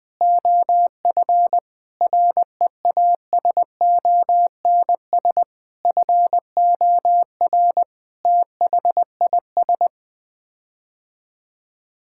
Morse code, heard as a keyed tone, about 20 wpm, 700 Hz. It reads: OF REASONS FOR THIS
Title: Sets of 8 Words - 20wpm